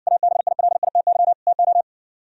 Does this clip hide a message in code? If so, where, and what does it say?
Yes: Morse code from 0.1 to 1.8 s, heard as a keyed tone, about 59 wpm, 700 Hz. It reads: ABILITY TO